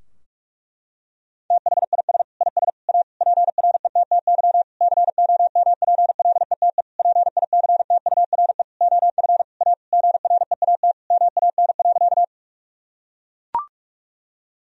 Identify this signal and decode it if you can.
Morse code, heard as a keyed tone, about 43 wpm, 700 Hz. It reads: THIS IS A PRETTY COMPLETE PICTURE OF A GREAT MAN.
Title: Sentences from Top 1000 Words - Review Part 2 - 45wpm